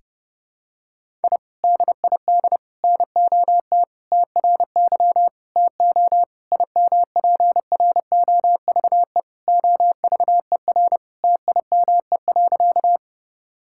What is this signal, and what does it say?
Morse code, heard as a keyed tone, about 30 wpm, 700 Hz. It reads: I DID NOT TRY TO IMPROVE OVER TIME.
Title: Sentences from Top 1000 Words - Review Part 5 - 30wpm